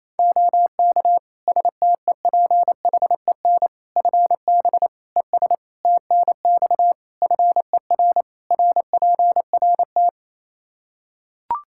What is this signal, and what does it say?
Morse code, heard as a keyed tone, about 28 wpm, 700 Hz. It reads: OK STEPHEN FB ES TNX FER RPRT
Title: Practice QSO - Exchange 2 - Info - 28wpm